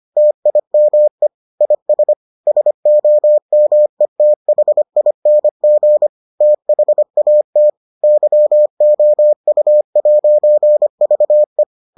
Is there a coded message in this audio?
Yes: Morse code from 0.2 to 11.6 s, heard as a keyed tone, about 25 wpm, 600 Hz. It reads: TIME IS SOMETHING THAT YOU'VE